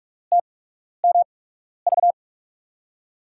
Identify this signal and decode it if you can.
Morse code, heard as a keyed tone, about 46 wpm, spaced as for 12 wpm, 700 Hz. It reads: TMV
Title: USA Call Signs - 5 Characters - 8x Character Spacing - 45wpm